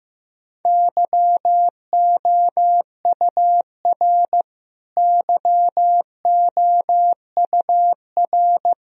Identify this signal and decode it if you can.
Morse code, heard as a keyed tone, about 15 wpm, 700 Hz. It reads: YOUR YOUR
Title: Top 100 Words Repeated 15wpm